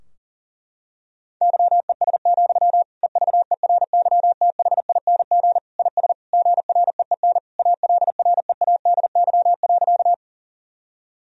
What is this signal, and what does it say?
Morse code, heard as a keyed tone, about 40 wpm, 700 Hz. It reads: YES, EVERYTHING IS GREEN ALREADY.
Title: Sentences from Top 1000 Words - Review Part 2 - 40wpm